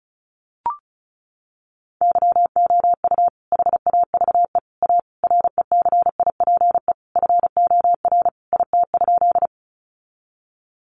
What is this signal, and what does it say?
Morse code, heard as a keyed tone, about 34 wpm, 700 Hz. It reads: YOU HAVE A RECIPE FOR IT?
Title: Sentences from Top 1200 Words 35wpm